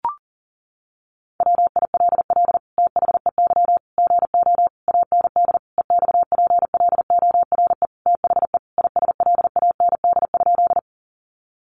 Morse code, heard as a keyed tone, about 40 wpm, 700 Hz. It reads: WILL THEY GO AND EXPLORE THE ISLAND?
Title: Sentences from Top 1100 Words 40wpm